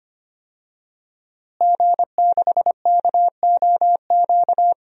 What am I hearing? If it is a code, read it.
Morse code, heard as a keyed tone, about 25 wpm, 700 Hz. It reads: G6KOQ
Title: DXCC-100 Call Signs - 5 Characters - Repeated 2x - 25wpm